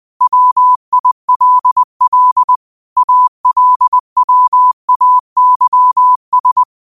Morse code, heard as a keyed tone, about 20 wpm, 1000 Hz. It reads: WILL ALWAYS
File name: qotdfiles / 20wpm